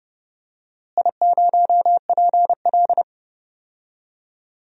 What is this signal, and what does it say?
Morse code, heard as a keyed tone, about 30 wpm, 700 Hz. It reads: I0PL
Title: DXCC-100 Call Signs - 4 Characters - Repeated 3x - 30wpm